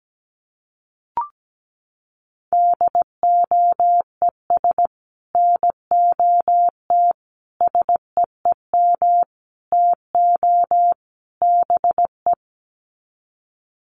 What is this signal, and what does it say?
Morse code, heard as a keyed tone, about 17 wpm, 700 Hz. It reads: DOES NOT SEEM TO BE
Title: Sets of 5 Words 17wpm